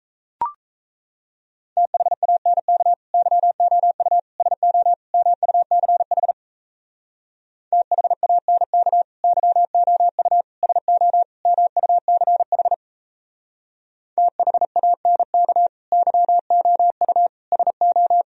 Morse code, heard as a keyed tone, about 37 wpm, 700 Hz. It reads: THANK YOU SO MUCH THANK YOU SO MUCH THANK YOU SO